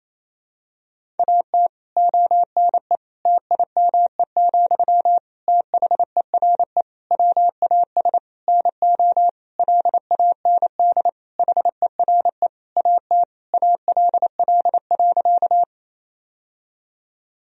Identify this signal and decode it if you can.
Morse code, heard as a keyed tone, about 28 wpm, 700 Hz. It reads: AT ONE TIME, THERE WAS NO LAND HERE AT ALL.
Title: Sentences from Top 1000 Words - Review Part 1 - 28wpm